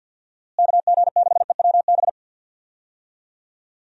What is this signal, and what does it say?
Morse code, heard as a keyed tone, about 51 wpm, 700 Hz. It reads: KG6EWB
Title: USA Call Signs - 6 Characters - Repeated 3x - 50wpm